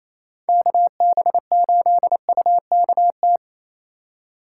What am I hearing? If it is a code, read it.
Morse code, heard as a keyed tone, about 36 wpm, 700 Hz. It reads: KB8UKT